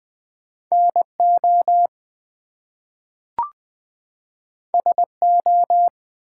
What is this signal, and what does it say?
Morse code, heard as a keyed tone, about 20 wpm, 700 Hz. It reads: NO SO
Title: Two Letter Words - 20wpm